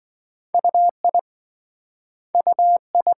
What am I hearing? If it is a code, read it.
Morse code, heard as a keyed tone, about 26 wpm, 700 Hz. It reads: UI UI